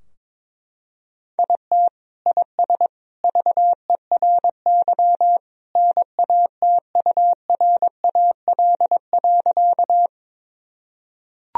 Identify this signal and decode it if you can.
Morse code, heard as a keyed tone, about 22 wpm, 700 Hz. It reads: IT IS VERY NATURAL.
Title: Sentences from Top 1000 Words - Review Part 4 - 22wpm